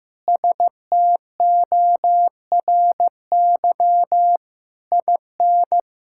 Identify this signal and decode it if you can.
Morse code, heard as a keyed tone, about 16 wpm, 700 Hz. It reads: STORY IN
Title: Sentences from Top 1000 Words - Review Part 5 - 15wpm